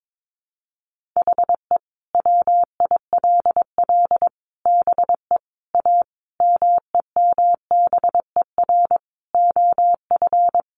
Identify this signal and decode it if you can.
Morse code, heard as a keyed tone, about 22 wpm, 700 Hz. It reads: HE WILL BE A MEMBER OF
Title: Sentences from Top 1100 Words 22wpm